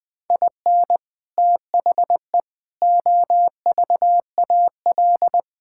Morse code, heard as a keyed tone, about 21 wpm, 700 Hz. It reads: IN THE OVAL
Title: Sets of 8 Words - 20wpm